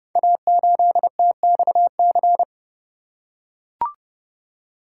Morse code, heard as a keyed tone, about 31 wpm, 700 Hz. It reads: A8TXC
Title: Code Groups - 5 Letter:Numbers - 30wpm